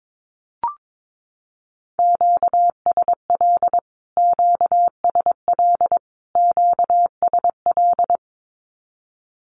Morse code, heard as a keyed tone, about 22 wpm, 700 Hz. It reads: QSL QSL QSL
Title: Q-Codes Repeated 3x 22wpm